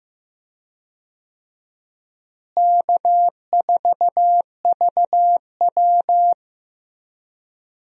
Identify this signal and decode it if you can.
Morse code, heard as a keyed tone, about 15 wpm, 700 Hz. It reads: K4VW
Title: USA Call Signs - 4 Characters - Repeated 2x - 15wpm